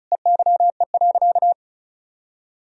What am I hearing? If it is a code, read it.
Morse code, heard as a keyed tone, about 34 wpm, 700 Hz. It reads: EYE.
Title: Sentences from Top 1000 Words - Review Part 1 - 35wpm